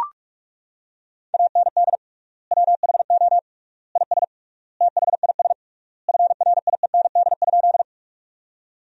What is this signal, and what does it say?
Morse code, heard as a keyed tone, about 41 wpm, 700 Hz. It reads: AND WHO IS THIS FRIEND?
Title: 3x Word-Spacing - Sentences from Top 300 Words - 45wpm